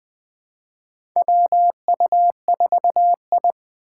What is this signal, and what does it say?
Morse code, heard as a keyed tone, about 20 wpm, 700 Hz. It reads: WU4I
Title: USA Call Signs - 4 Characters - 6x Character Spacing - 20wpm